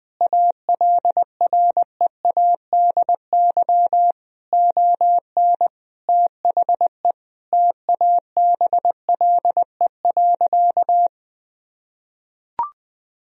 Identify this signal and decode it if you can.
Morse code, heard as a keyed tone, about 20 wpm, 700 Hz. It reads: ALREADY ON THE TABLE.
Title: Sentences from Top 1000 Words - Review Part 3 - 20wpm